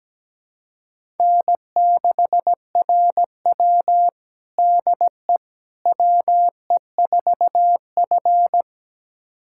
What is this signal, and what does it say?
Morse code, heard as a keyed tone, about 17 wpm, 700 Hz. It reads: N6RW DE WE4F
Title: USA Callsign DE Callsign - 4 Characters - 17wpm